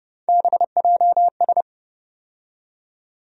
Morse code, heard as a keyed tone, about 30 wpm, 700 Hz. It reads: BJS
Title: USA Call Sign Postfixes - 3 Letters 30wpm